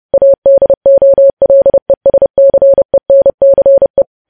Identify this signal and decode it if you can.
Morse code, heard as a keyed tone, about 30 wpm, 550 Hz. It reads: ADOLESCENCE
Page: news headlines in morse code at 30 wpm